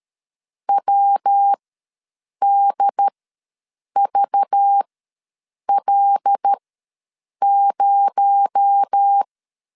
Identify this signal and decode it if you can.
Morse code, heard as a keyed tone, about 13 wpm, spaced as for 7 wpm, 800 Hz. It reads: WDVL0
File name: LEZIONI / 7 GF9Z / 7 G ws 12wpm 3140